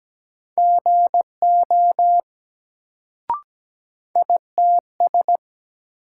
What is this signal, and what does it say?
Morse code, heard as a keyed tone, about 17 wpm, 700 Hz. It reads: GO ITS
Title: Sets of 2 Words - New Word Drill - 17wpm